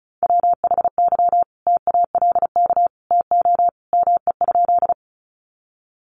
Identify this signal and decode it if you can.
Morse code, heard as a keyed tone, about 34 wpm, 700 Hz. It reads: WHY TALK TO ME?
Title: Sentences from Top 400 Words 35wpm